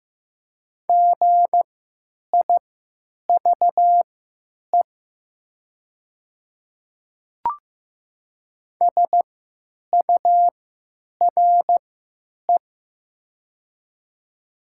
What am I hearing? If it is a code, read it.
Morse code, heard as a keyed tone, about 15 wpm, spaced as for 9 wpm, 700 Hz. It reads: GIVE SURE
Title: 3x Character Spacing - Top 300 Words - 15wpm